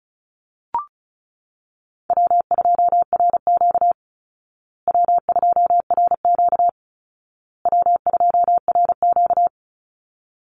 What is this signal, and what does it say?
Morse code, heard as a keyed tone, about 34 wpm, 700 Hz. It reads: W2RQ W2RQ W2RQ
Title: Easy Call Signs Repeated 3x - 35wpm